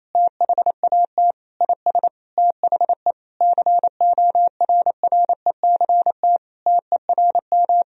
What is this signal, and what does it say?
Morse code, heard as a keyed tone, about 28 wpm, 700 Hz. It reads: THAT IS THE CORRECT TERM
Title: Sentences from Top 1000 Words - Review Part 4 - 28wpm